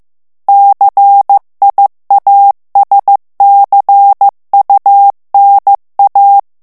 Morse code, heard as a keyed tone, about 15 wpm, 800 Hz. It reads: CIASCUNA
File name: morse mp3 / ITALIEN texte clair 2 vitesse 14 mots minute